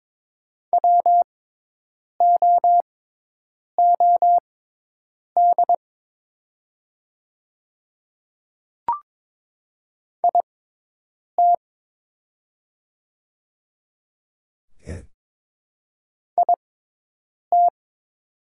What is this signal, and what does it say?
Morse code, heard as a keyed tone, about 22 wpm, spaced as for 8 wpm, 690 Hz. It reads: WOOD IT IT
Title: 6x Character Spacing - Top 300 Words - 22wpm